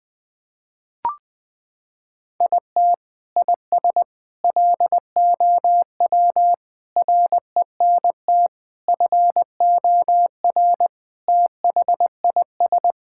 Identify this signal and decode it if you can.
Morse code, heard as a keyed tone, about 21 wpm, 700 Hz. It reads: IT IS LOW RENT FOR THIS
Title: Sentences from Top 1000 Words - Review Part 5 - 20wpm